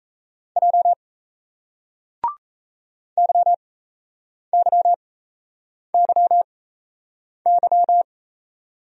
Morse code, heard as a keyed tone, about 37 wpm, spaced as for 8 wpm, 700 Hz. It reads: J YYYY